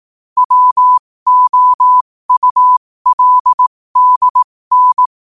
Morse code, heard as a keyed tone, about 18 wpm, spaced as for 15 wpm, 1000 Hz. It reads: WOULDN